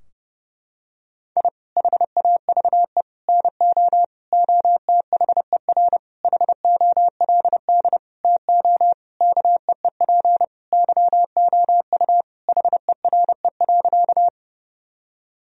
Morse code, heard as a keyed tone, about 30 wpm, 700 Hz. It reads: I HAVE NO OTHER HOLD TO KEEP YOU HERE.
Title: Sentences from Top 1000 Words - Review Part 2 - 30wpm